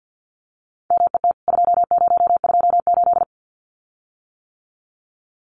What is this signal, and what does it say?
Morse code, heard as a keyed tone, about 48 wpm, 700 Hz. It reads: MET 2028
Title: CWT Exchange 50wpm